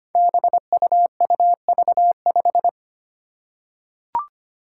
Morse code, heard as a keyed tone, about 25 wpm, 700 Hz. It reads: BUUV5
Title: Five Challenging Letter-Number Combinations 25wpm